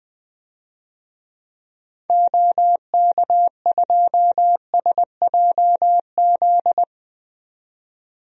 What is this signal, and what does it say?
Morse code, heard as a keyed tone, about 21 wpm, 700 Hz. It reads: OK2SJZ
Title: DXCC-100 Call Signs - 6 Characters - 20wpm